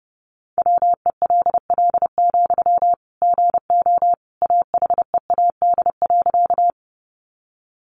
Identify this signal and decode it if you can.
Morse code, heard as a keyed tone, about 30 wpm, 700 Hz. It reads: WELL, GO AHEAD.
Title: Sentences from Top 1100 Words 30wpm